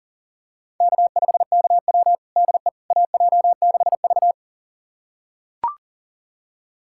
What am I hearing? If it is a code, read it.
Morse code, heard as a keyed tone, about 37 wpm, 700 Hz. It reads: K5KW DE AJ6V
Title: USA Callsign DE Callsign - 4 Characters - 40wpm